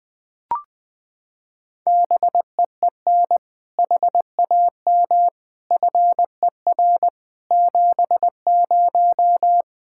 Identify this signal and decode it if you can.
Morse code, heard as a keyed tone, about 20 wpm, 700 Hz. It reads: BEEN HAM FER 70
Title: Practice QSO - Exchange 3 - More Info - 20wpm